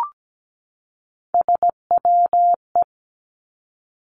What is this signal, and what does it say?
Morse code, heard as a keyed tone, about 17 wpm, 700 Hz. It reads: SWE
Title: Three Challenging Letter-Number Combinations 17wpm